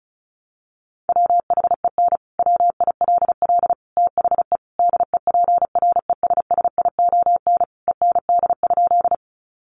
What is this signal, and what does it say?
Morse code, heard as a keyed tone, about 34 wpm, 700 Hz. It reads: WHEN WILL THE DEPRESSION END?
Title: Sentences from Top 1300 Words 35wpm